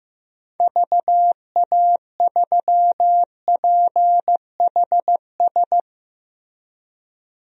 Morse code, heard as a keyed tone, about 15 wpm, 700 Hz. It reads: VA3PHS
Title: DXCC-100 Call Signs - 6 Characters - Repeated 3x - 15wpm